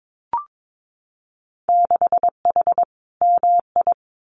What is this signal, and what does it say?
Morse code, heard as a keyed tone, about 22 wpm, 700 Hz. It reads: 6H MI
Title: Winter Field Day Exchanges Encore 22wpm